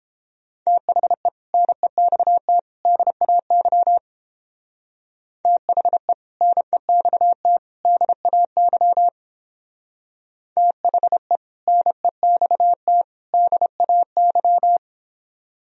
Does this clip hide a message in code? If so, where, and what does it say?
Yes: Morse code from 0.7 to 14.8 s, heard as a keyed tone, about 29 wpm, 700 Hz. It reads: THE NEXT DAY THE NEXT DAY THE NEXT DAY